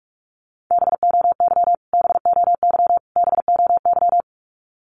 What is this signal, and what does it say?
Morse code, heard as a keyed tone, about 44 wpm, 700 Hz. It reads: BOY BOY BOY
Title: Top 200 Words Repeated 45wpm